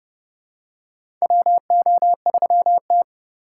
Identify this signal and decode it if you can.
Morse code, heard as a keyed tone, about 30 wpm, 700 Hz. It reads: WO3T
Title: USA Call Signs - 4 Characters - 3x Character Spacing - 30wpm